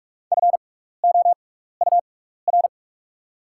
Morse code, heard as a keyed tone, about 48 wpm, spaced as for 16 wpm, 690 Hz. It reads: FOUR
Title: 6x Character Spacing - Top 500 Words - 45wpm